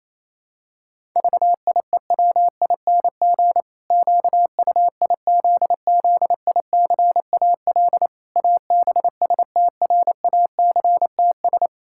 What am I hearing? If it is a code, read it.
Morse code, heard as a keyed tone, about 28 wpm, 700 Hz. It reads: VIEWING QUIZZICAL ABSTRACTS
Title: Pangram Challenge v2 - 28wpm